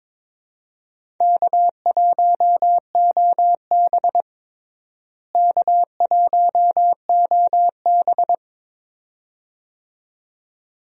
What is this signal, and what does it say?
Morse code, heard as a keyed tone, about 22 wpm, 700 Hz. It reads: K1OB K1OB
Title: USA Call Signs - 4 Characters - Repeated 2x - 22wpm